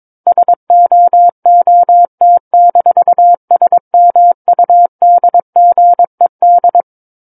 Morse code, heard as a keyed tone, about 22 wpm, 700 Hz. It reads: SOOT-SMUDGED
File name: Book One / 13-Chapter 13 - 22 WPM